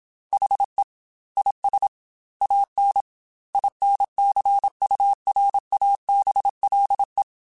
Morse code, heard as a keyed tone, about 27 wpm, 800 Hz. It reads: HE IS AN INCURABLE
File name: Quotes / Quotes-mp3s / Optimist @25